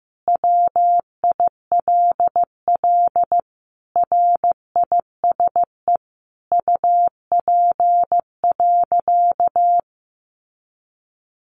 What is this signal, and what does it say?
Morse code, heard as a keyed tone, about 15 wpm, 700 Hz. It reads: WILL RISE UP.
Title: Sentences from Top 900 Words 15wpm